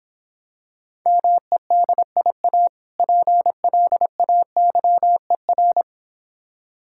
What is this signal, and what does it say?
Morse code, heard as a keyed tone, about 26 wpm, 690 Hz. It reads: MEDIA PLAYER